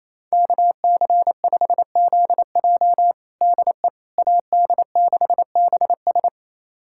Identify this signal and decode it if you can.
Morse code, heard as a keyed tone, about 28 wpm, 700 Hz. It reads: KC5ZJ DE AD6BS
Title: USA Callsign DE Callsign - 5 Characters - 28wpm